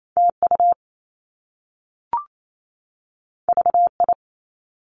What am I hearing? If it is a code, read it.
Morse code, heard as a keyed tone, about 28 wpm, 700 Hz. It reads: TU VI
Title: Top Two Letter Combinations in Words 28wpm